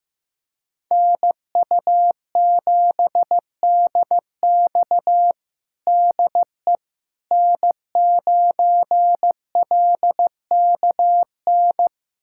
Morse code, heard as a keyed tone, about 15 wpm, 700 Hz. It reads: NU7DX DE N9LKN
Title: USA Callsign DE Callsign - 5 Characters - 15wpm